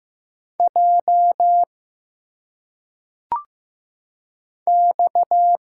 Morse code, heard as a keyed tone, about 15 wpm, 700 Hz. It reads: J X
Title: Single Letters - Phonetic Pronunciation 15wpm